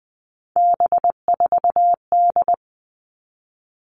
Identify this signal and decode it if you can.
Morse code, heard as a keyed tone, about 20 wpm, 700 Hz. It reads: B4D
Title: All Cut Numbers - 3-Digit 20wpm